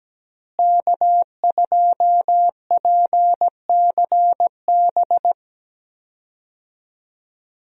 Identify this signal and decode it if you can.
Morse code, heard as a keyed tone, about 17 wpm, 700 Hz. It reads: K2PCB
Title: USA Call Signs - 5 Characters - Repeated 2x - 17wpm